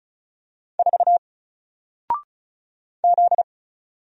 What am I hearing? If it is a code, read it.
Morse code, heard as a keyed tone, about 35 wpm, 700 Hz. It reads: 4 Z
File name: Single-Letter-Number-35wpm